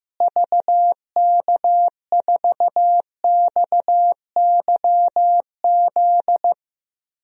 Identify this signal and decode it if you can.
Morse code, heard as a keyed tone, about 15 wpm, 700 Hz. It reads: VK4XYZ
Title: DXCC-100 Call Signs - 6 Characters - Repeated 3x - 15wpm